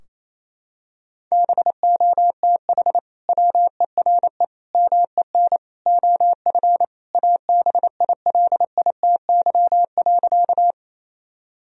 Morse code, heard as a keyed tone, about 28 wpm, 700 Hz. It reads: BOTH WERE MEN OF ABILITY.
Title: Sentences from Top 1000 Words - Review Part 3 - 28wpm